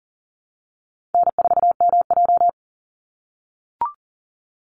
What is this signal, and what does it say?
Morse code, heard as a keyed tone, about 40 wpm, 700 Hz. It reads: N4MJ
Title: NA SOTA Chaser Callsigns 40wpm